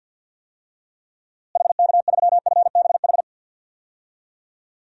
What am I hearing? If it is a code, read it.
Morse code, heard as a keyed tone, about 48 wpm, 700 Hz. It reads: HK3FBH
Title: DXCC-100 Call Signs - 6 Characters - Repeated 2x - 50wpm